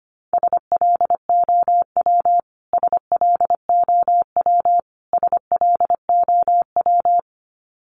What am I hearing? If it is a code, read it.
Morse code, heard as a keyed tone, about 25 wpm, 700 Hz. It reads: SLOW SLOW SLOW
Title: Top 500 Words Repeated 25wpm